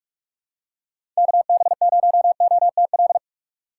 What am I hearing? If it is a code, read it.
Morse code, heard as a keyed tone, about 43 wpm, 700 Hz. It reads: KB0OTL
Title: USA Call Signs - 6 Characters - 2x Character Spacing - 45wpm